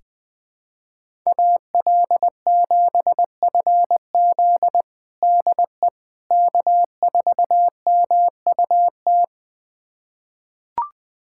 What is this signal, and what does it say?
Morse code, heard as a keyed tone, about 20 wpm, 700 Hz. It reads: AL7FZ DE K4MUT
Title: USA Callsign DE Callsign - 5 Characters - 20wpm